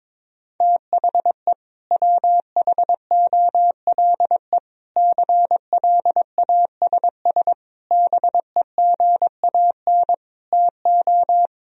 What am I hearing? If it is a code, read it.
Morse code, heard as a keyed tone, about 22 wpm, 700 Hz. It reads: THE WHOLE CLASS BEGAN TO
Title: Sentences from Top 1000 Words - Review Part 2 - 22wpm